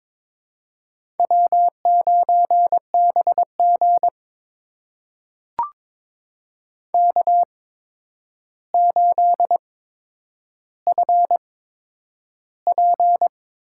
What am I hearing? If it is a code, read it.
Morse code, heard as a keyed tone, about 22 wpm, spaced as for 9 wpm, 700 Hz. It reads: W9BG K8FP
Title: USA Call Signs - 4 Characters - 8x Character Spacing - 22wpm